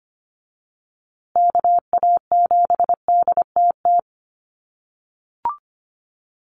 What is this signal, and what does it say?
Morse code, heard as a keyed tone, about 25 wpm, 700 Hz. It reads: KA7DTT
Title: USA Call Signs Repeated 2x 25wpm